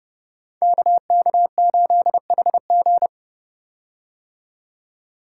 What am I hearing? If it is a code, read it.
Morse code, heard as a keyed tone, about 30 wpm, 700 Hz. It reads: KK8HG
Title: Code Groups - 5 Letter:Numbers - 30wpm